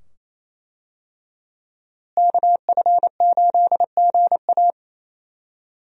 Morse code, heard as a keyed tone, about 28 wpm, 700 Hz. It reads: KF8GA
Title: USA Call Signs - 5 Characters - Repeated 2x - 28wpm